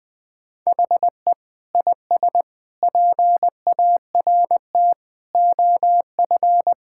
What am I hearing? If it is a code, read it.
Morse code, heard as a keyed tone, about 20 wpm, 700 Hz. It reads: HE IS PART OF
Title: Sentences with 3-Word Phrases - Spoken Beginning - 20wpm